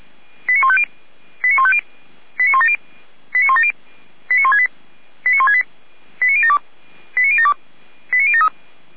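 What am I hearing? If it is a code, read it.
Five-tone selective call (ZVEI), heard as quick runs of five tones, tones of about 70 ms. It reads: 89150 89150 89170 89170 89168 89168 89072 89072 89073
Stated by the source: ZVEI five-tone selective call received by SDR.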